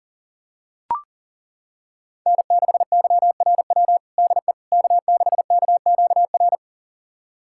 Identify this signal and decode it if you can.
Morse code, heard as a keyed tone, about 40 wpm, 700 Hz. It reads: N6YRW DE K6KQR
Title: USA Callsign DE Callsign - 5 Characters - 40wpm